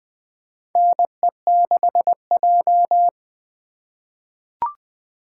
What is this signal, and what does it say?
Morse code, heard as a keyed tone, about 21 wpm, 700 Hz. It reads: NE6J
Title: USA Call Signs - 3x Character Spacing - 20wpm